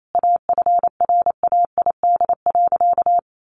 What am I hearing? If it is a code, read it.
Morse code, heard as a keyed tone, about 28 wpm, 700 Hz. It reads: AFRAID.
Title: Sentences from Top 1500 Words 28wpm